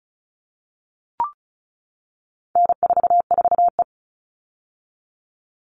Morse code, heard as a keyed tone, about 34 wpm, 700 Hz. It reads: N44E
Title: All Cut Numbers - 4-Digit 35wpm